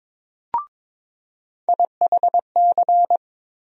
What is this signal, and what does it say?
Morse code, heard as a keyed tone, about 22 wpm, 700 Hz. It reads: IHC
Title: USA Call Sign Postfixes - 3 Letters 22wpm